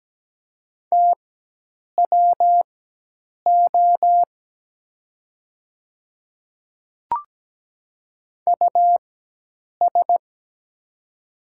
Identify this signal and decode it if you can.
Morse code, heard as a keyed tone, about 17 wpm, spaced as for 8 wpm, 700 Hz. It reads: TWO US
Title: 4x Character Spacing - Top 200 Words - 17wpm